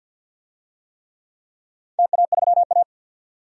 Morse code, heard as a keyed tone, about 51 wpm, 700 Hz. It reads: TA3A
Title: DXCC-100 Call Signs - 4 Characters - 50wpm